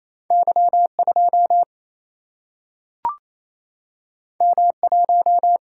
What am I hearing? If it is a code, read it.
Morse code, heard as a keyed tone, about 28 wpm, 700 Hz. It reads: Y 2 M 1
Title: Single-Letter Single-Number 28wpm